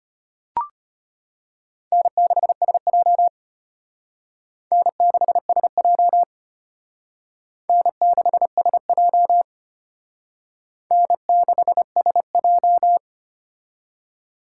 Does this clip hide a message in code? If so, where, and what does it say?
Yes: Morse code from 1.9 to 13.0 s, heard as a keyed tone, about 32 wpm, 700 Hz. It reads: N6SJ N6SJ N6SJ N6SJ